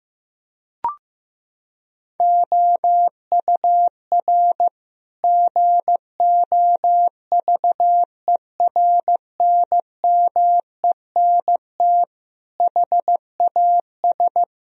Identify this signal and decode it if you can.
Morse code, heard as a keyed tone, about 16 wpm, 700 Hz. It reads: OUR GOVERNMENT HAS
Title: Sentences from Top 1000 Words - Review Part 4 - 15wpm